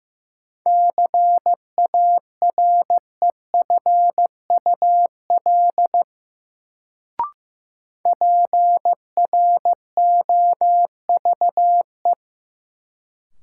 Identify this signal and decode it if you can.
Morse code, heard as a keyed tone, about 15 wpm, 700 Hz. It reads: CAREFUL PROVE
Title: Sentences from Top 1600 Words - New Word Drill Alt - 15wpm